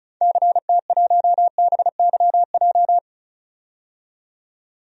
Morse code, heard as a keyed tone, about 34 wpm, 700 Hz. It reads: CT1BYJ
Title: DXCC-100 Call Signs - 6 Characters - 35wpm